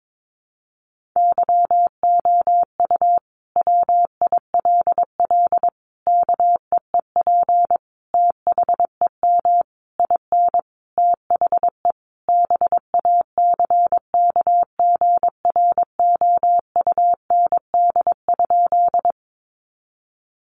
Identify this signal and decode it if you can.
Morse code, heard as a keyed tone, about 22 wpm, 700 Hz. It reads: YOU WILL KEEP THEM IN THE BACKGROUND?
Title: Sentences from Top 1300 Words 22wpm